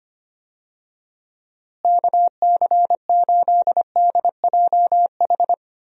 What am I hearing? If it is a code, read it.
Morse code, heard as a keyed tone, about 25 wpm, 700 Hz. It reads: KC8DJH
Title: USA Call Signs - 6 Characters - Repeated 3x - 25wpm